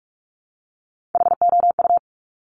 Morse code, heard as a keyed tone, about 46 wpm, 700 Hz. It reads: HOU
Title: Top Three Letter Combinations in Words 45wpm